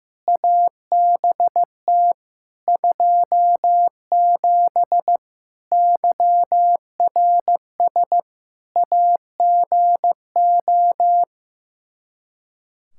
Morse code, heard as a keyed tone, about 16 wpm, 700 Hz. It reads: ABT 27 YRS AGO
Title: Practice QSO - Exchange 3 - More Info - 15wpm